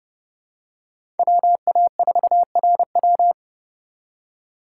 Morse code, heard as a keyed tone, about 30 wpm, 700 Hz. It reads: WA4RW
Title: USA Call Signs - 5 Characters - 3x Character Spacing - 30wpm